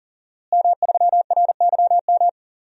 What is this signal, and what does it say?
Morse code, heard as a keyed tone, about 40 wpm, 690 Hz. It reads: M3RYM
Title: DXCC-100 Call Signs - 5 Characters - Repeated 2x - 40wpm